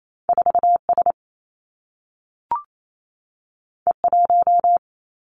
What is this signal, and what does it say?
Morse code, heard as a keyed tone, about 28 wpm, 700 Hz. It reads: 4 S E 1
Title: Two Challenging Letter-Number Combinations 28wpm